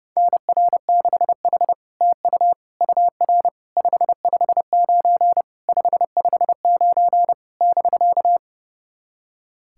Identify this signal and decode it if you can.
Morse code, heard as a keyed tone, about 29 wpm, 700 Hz. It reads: NR6H TU UR 559 559 <BK>
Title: SOTA QSO Practice 30wpm